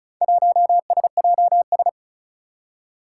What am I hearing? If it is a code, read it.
Morse code, heard as a keyed tone, about 36 wpm, 700 Hz. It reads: 1SJS
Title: Four Challenging Letter-Number Combinations 35wpm